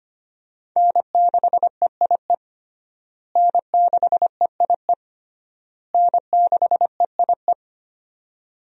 Morse code, heard as a keyed tone, about 25 wpm, 700 Hz. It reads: N6EIE N6EIE N6EIE
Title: USA Call Signs - 5 Characters - Repeated 3x - 25wpm